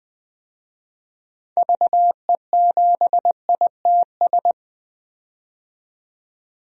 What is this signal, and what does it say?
Morse code, heard as a keyed tone, about 20 wpm, 700 Hz. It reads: VE7ITS
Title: DXCC-100 Call Signs - 6 Characters - 20wpm